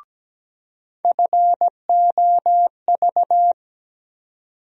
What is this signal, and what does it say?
Morse code, heard as a keyed tone, about 17 wpm, 700 Hz. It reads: FOV
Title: USA Call Sign Postfixes - 3 Letters 17wpm